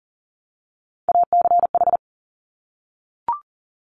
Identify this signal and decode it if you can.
Morse code, heard as a keyed tone, about 41 wpm, 690 Hz. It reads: ACH
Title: Top Three Letter Combinations in Words 40wpm